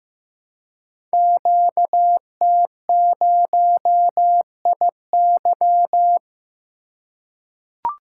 Morse code, heard as a keyed tone, about 15 wpm, 700 Hz. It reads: QT0IY
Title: Code Groups - 5 Letter:Numbers - 15wpm